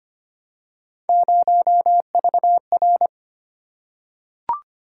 Morse code, heard as a keyed tone, about 26 wpm, 700 Hz. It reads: 0VR
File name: Three Letter-Number Combinations - 25wpm